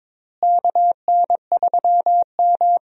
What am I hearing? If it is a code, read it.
Morse code, heard as a keyed tone, about 22 wpm, 700 Hz. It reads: KN3M
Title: USA Call Signs - 4 Characters - Repeated 2x - 22wpm